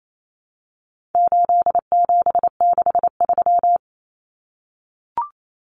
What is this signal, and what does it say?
Morse code, heard as a keyed tone, about 28 wpm, 700 Hz. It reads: 8763
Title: Common Cut Numbers - 4-Digit 28wpm